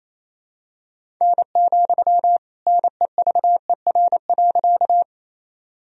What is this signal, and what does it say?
Morse code, heard as a keyed tone, about 28 wpm, 700 Hz. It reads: N, NEVER.
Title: Sentences from Top 300 Words and Abbreviations 28wpm